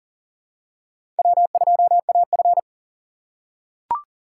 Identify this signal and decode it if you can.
Morse code, heard as a keyed tone, about 40 wpm, 700 Hz. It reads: W2AF
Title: USA Call Signs - 3x Character Spacing - 40wpm